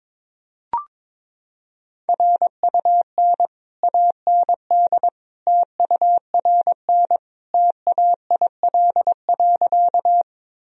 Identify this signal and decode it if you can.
Morse code, heard as a keyed tone, about 22 wpm, 700 Hz. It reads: RUN AND TURN TAIL.
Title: Sentences from Top 1000 Words - Review Part 3 - 22wpm